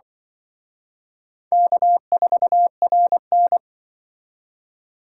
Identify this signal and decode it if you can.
Morse code, heard as a keyed tone, about 25 wpm, 700 Hz. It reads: K4RN
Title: Speed Racing - USA Call Signs 20wpm